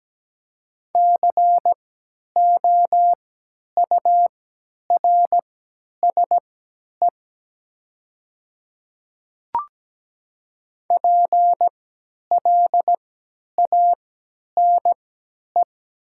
Morse code, heard as a keyed tone, about 17 wpm, spaced as for 10 wpm, 700 Hz. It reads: COURSE PLANE
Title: 3x Character Spacing - Top 500 Words - 17wpm